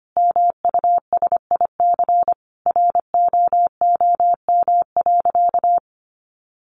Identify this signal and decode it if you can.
Morse code, heard as a keyed tone, about 25 wpm, 700 Hz. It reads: MUSIC ROOM.
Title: Sentences from Top 700 Words 25wpm